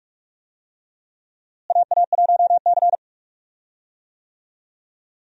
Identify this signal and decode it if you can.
Morse code, heard as a keyed tone, about 45 wpm, 700 Hz. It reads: AA1C
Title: USA Call Signs - 4 Characters - Repeated 2x - 45wpm